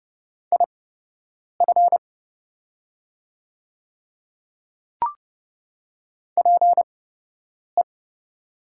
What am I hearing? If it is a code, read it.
Morse code, heard as a keyed tone, about 30 wpm, spaced as for 8 wpm, 700 Hz. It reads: IF PE
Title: 8x Character Spacing - Top 100 Words - 30wpm